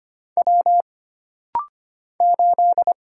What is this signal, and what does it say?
Morse code, heard as a keyed tone, about 25 wpm, 700 Hz. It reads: W 8
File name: Single Character (Letter:Number:Punctuation:Prosign) - Rapid-Fire - 25wpm